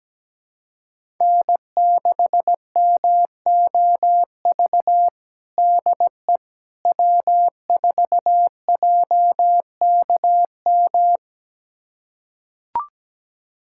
Morse code, heard as a keyed tone, about 17 wpm, 700 Hz. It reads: N6MOV DE W4JKM
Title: USA Callsign DE Callsign - 5 Characters - 17wpm